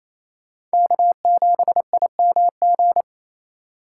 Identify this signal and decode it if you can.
Morse code, heard as a keyed tone, about 36 wpm, 700 Hz. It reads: K7IMG